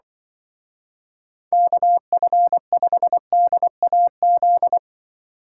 Morse code, heard as a keyed tone, about 25 wpm, 700 Hz. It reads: KF5DAZ
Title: Speed Racing - USA 2x3 Call Signs 20wpm